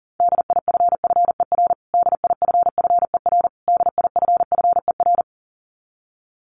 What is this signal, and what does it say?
Morse code, heard as a keyed tone, about 40 wpm, 700 Hz. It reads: DIFFER DIFFER DIFFER
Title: Top 500 Words Repeated 40wpm